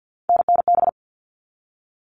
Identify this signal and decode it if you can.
Morse code, heard as a keyed tone, about 50 wpm, 700 Hz. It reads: NNB
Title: All Cut Numbers - 3-Digit 50wpm